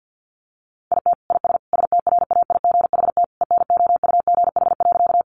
Morse code, heard as a keyed tone, about 50 wpm, 700 Hz. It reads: IT IS STRAIGHT ENOUGH.
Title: Sentences from Top 1000 Words 50wpm